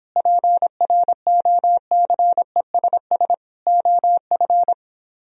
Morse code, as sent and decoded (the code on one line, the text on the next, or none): .--. .-. --- -.-. ... ... / --- ..-.
PROCESS OF